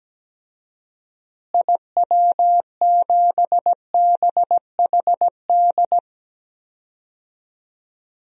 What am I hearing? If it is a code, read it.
Morse code, heard as a keyed tone, about 17 wpm, 700 Hz. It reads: IW7BHD
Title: DXCC-100 Call Signs - 6 Characters - Repeated 3x - 17wpm